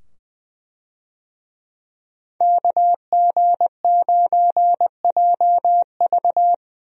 Morse code, heard as a keyed tone, about 20 wpm, 700 Hz. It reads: KG9JV
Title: USA Call Signs - 5 Characters - 20wpm